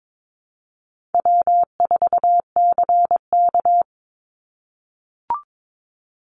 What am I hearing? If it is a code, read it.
Morse code, heard as a keyed tone, about 22 wpm, 700 Hz. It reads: W4CK
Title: Easy Call Signs - 22wpm